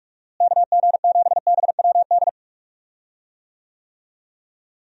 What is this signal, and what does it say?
Morse code, heard as a keyed tone, about 43 wpm, 700 Hz. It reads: KG7BWD
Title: USA Call Signs - 6 Characters - Repeated 3x - 45wpm